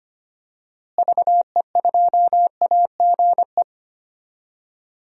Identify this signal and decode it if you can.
Morse code, heard as a keyed tone, about 25 wpm, 700 Hz. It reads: VE2AGE
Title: DXCC-100 Call Signs - 6 Characters - Repeated 3x - 25wpm